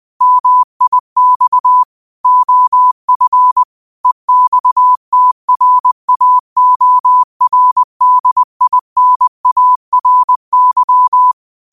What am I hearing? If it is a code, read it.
Morse code, heard as a keyed tone, about 20 wpm, 1000 Hz. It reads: MIX OF EXTRAORDINARY